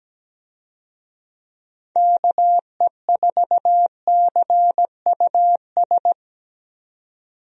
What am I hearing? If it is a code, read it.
Morse code, heard as a keyed tone, about 17 wpm, 700 Hz. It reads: KE4CUS
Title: USA Call Signs - 6 Characters - 17wpm